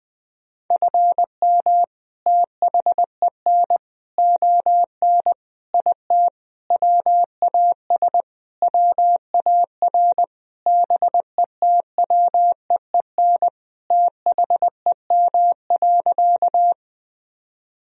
Morse code, heard as a keyed tone, about 20 wpm, 700 Hz. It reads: FM THEN ON IT WAS WAR BETWEEN THEM.
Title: Sentences from Top 400 Words and Abbreviations 20wpm